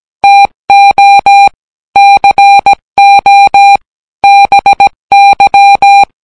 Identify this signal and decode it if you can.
Morse code, heard as a keyed tone, about 17 wpm, 800 Hz. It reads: TO CO BY